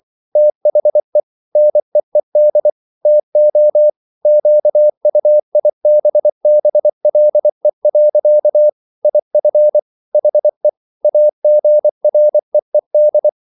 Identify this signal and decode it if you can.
Morse code, heard as a keyed tone, about 24 wpm, 600 Hz. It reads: THE NEED TO QUIBBLE. IF HE AGREED